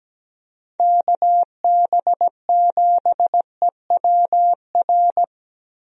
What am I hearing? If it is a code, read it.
Morse code, heard as a keyed tone, about 17 wpm, 700 Hz. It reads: KB7EWR
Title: USA Call Signs - 6 Characters - Repeated 2x - 17wpm